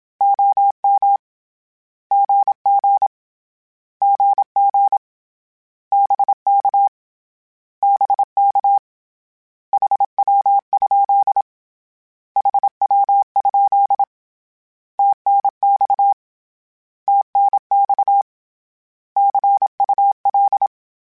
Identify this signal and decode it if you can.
Morse code, heard as a keyed tone, about 27 wpm, 800 Hz. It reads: OM GG GG BK BK HW? HW? TNX TNX CUL
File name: Abreviations / Abreviations mp3s / Abreviations@25